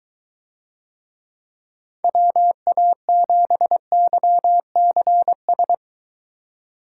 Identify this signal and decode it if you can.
Morse code, heard as a keyed tone, about 21 wpm, 700 Hz. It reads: WA7YCS